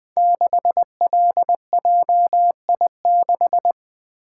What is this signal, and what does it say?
Morse code, heard as a keyed tone, about 20 wpm, 700 Hz. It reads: 6LJI6
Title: Code Groups - 5 Letter:Numbers - 20wpm